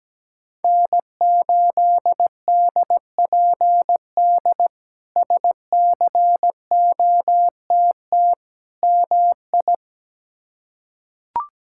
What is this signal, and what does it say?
Morse code, heard as a keyed tone, about 17 wpm, 700 Hz. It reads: N8DPD SCOTT MI
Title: K1USN SST QSO Practice - 17wpm